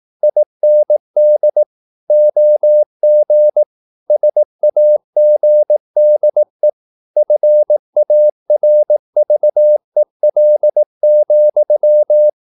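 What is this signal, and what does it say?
Morse code, heard as a keyed tone, about 18 wpm, 600 Hz. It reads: IND OG SAGDE FARVEL,
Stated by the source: Slow (QRS) Morse practice at 18 wpm, CW med Gnister.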